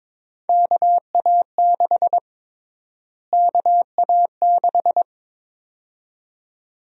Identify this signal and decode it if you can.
Morse code, heard as a keyed tone, about 21 wpm, 700 Hz. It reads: KA6 KA6
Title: USA Call Sign Prefixes - Repeated 2x - 22wpm